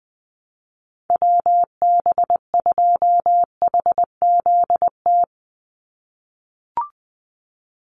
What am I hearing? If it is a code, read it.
Morse code, heard as a keyed tone, about 20 wpm, 700 Hz. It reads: WB2HZT
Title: USA Call Signs 20wpm